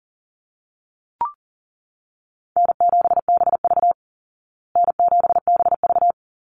Morse code, heard as a keyed tone, about 37 wpm, 700 Hz. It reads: N7BV N7BV
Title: Easy Call Signs Repeated 3x - 40wpm